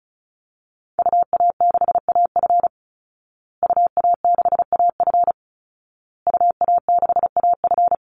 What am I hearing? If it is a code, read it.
Morse code, heard as a keyed tone, about 34 wpm, 700 Hz. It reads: UA6AF UA6AF UA6AF
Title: Call Signs Repeated 2x - 35wpm